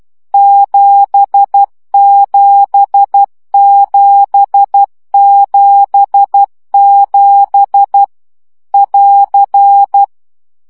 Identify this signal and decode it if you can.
Morse code, heard as a keyed tone, about 12 wpm, 800 Hz. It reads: 77777 <AR>
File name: CW gyakorlas 7 10 baud